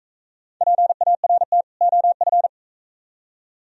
Morse code, heard as a keyed tone, about 44 wpm, 700 Hz. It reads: PART OF